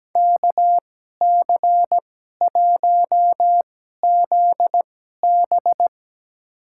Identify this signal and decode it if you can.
Morse code, heard as a keyed tone, about 17 wpm, spaced as for 12 wpm, 700 Hz. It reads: KC1ZB
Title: USA Call Signs - 5 Characters - 2x Character Spacing - 17wpm